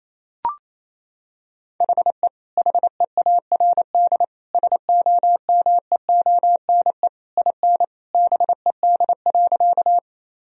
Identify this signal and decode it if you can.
Morse code, heard as a keyed tone, about 28 wpm, 700 Hz. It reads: HE HEARD SOMEONE IN BED.
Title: Sentences from Top 1000 Words - Review Part 4 - 28wpm